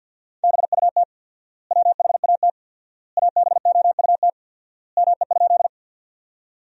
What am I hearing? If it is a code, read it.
Morse code, heard as a keyed tone, about 50 wpm, 700 Hz. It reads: BUT WHAT ABOUT ME?
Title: 4x Word-Spacing - Sentences from Top 200 Words - 50wpm